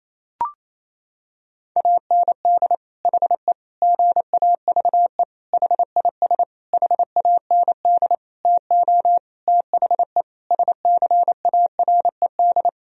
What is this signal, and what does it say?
Morse code, heard as a keyed tone, about 28 wpm, 700 Hz. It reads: AND HE GAVE HIS HAND TO THE SCARED
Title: Sentences from Top 1000 Words - Review Part 4 - 28wpm